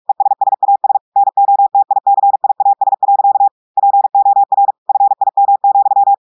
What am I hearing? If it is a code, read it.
Morse code, heard as a keyed tone, about 46 wpm, 800 Hz. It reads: ESSAS NOTICIAS. POR FIM,
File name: audios / portugais / portugais texte en clair 2 vitesse 45 mots minutes